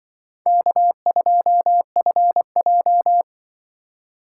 Morse code, heard as a keyed tone, about 23 wpm, 700 Hz. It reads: K2FJ
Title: Speed Racing - USA 1x2 Call Signs 20wpm